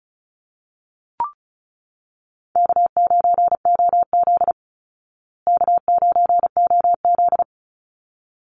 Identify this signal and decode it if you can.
Morse code, heard as a keyed tone, about 34 wpm, 700 Hz. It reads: K9OZ K9OZ
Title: Easy Call Signs Repeated 2x - 35wpm